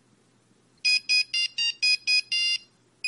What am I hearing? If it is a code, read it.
There is no code.